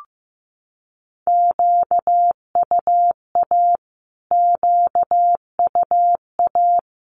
Morse code, heard as a keyed tone, about 15 wpm, 700 Hz. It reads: QUA QUA
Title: Q-Codes Repeated 3x 15wpm